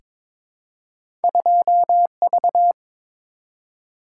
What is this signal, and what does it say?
Morse code, heard as a keyed tone, about 23 wpm, 700 Hz. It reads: 2V
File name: practice / 22wpm / Two Letter-Number Combinations - 22wpm